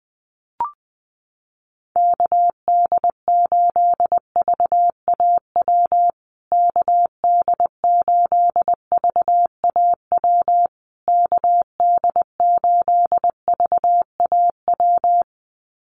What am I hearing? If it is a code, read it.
Morse code, heard as a keyed tone, about 20 wpm, 700 Hz. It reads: KD8VAW KD8VAW KD8VAW
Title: USA Call Signs Repeated 3x 20wpm